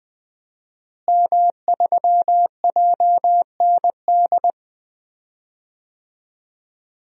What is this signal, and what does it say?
Morse code, heard as a keyed tone, about 20 wpm, 700 Hz. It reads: M3JND
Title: DXCC-100 Call Signs - 5 Characters - Repeated 3x - 20wpm